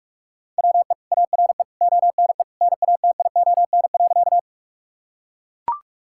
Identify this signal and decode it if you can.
Morse code, heard as a keyed tone, about 41 wpm, 700 Hz. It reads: WE ARE ONE NATION.
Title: Sentences from Top 1000 Words - Review Part 6 - 45wpm